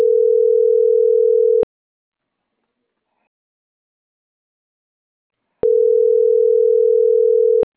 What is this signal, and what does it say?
Ringback tone (North America), 440 + 480 Hz.